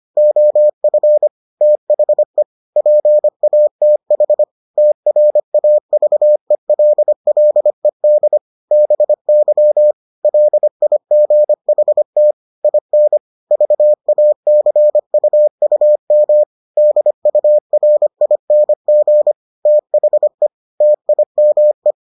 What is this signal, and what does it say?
Morse code, heard as a keyed tone, about 25 wpm, 600 Hz. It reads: OF THE PATH TRAVELLED BY LIGHT IN VACUUM DURING THE TIME